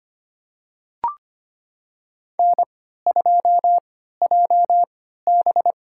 Morse code, heard as a keyed tone, about 25 wpm, spaced as for 14 wpm, 700 Hz. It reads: N2JB
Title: USA Call Signs - 3x Character Spacing - 25wpm